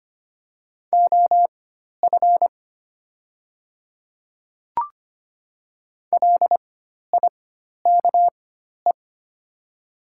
Morse code, heard as a keyed tone, about 25 wpm, spaced as for 12 wpm, 700 Hz. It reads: OF LIKE
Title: 4x Character Spacing - Top 100 Words - 25wpm